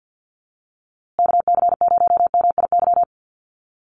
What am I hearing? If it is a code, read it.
Morse code, heard as a keyed tone, about 50 wpm, 700 Hz. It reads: KC0MIY
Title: USA Call Signs Repeated 3x 50wpm